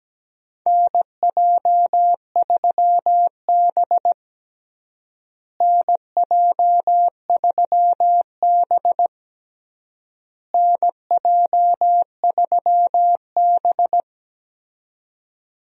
Morse code, heard as a keyed tone, about 17 wpm, 700 Hz. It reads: NJ3B NJ3B NJ3B
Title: USA Call Signs - 4 Characters - Repeated 3x - 17wpm